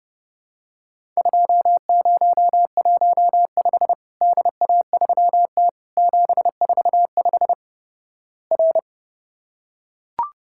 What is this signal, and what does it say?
Morse code, heard as a keyed tone, about 32 wpm, 700 Hz. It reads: 2015 DA3T 745 R
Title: Worked All Europe DX Contest - QTC Exchange - 30wpm